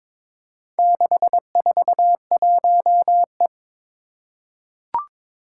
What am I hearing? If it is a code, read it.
Morse code, heard as a keyed tone, about 22 wpm, 700 Hz. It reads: 641E
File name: practice / 22wpm / Four Letter-Number Combinations - 22wpm